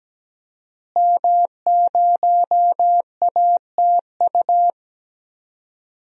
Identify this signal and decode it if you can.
Morse code, heard as a keyed tone, about 17 wpm, 700 Hz. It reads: M0ATU
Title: DXCC-100 Call Signs - 5 Characters - Repeated 3x - 17wpm